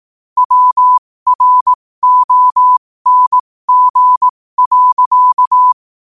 Morse code, heard as a keyed tone, about 18 wpm, spaced as for 15 wpm, 1000 Hz. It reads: WRONG.
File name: qotdfiles / 15wpm